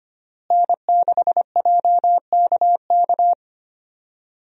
Morse code, heard as a keyed tone, about 25 wpm, 700 Hz. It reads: N6JKK
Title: USA Call Signs - 5 Characters - Repeated 3x - 25wpm